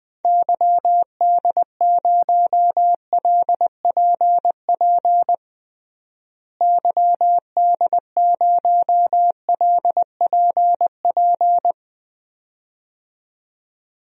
Morse code, heard as a keyed tone, about 20 wpm, 700 Hz. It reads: YD0LPP YD0LPP
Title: DXCC-100 Call Signs - 6 Characters - Repeated 2x - 20wpm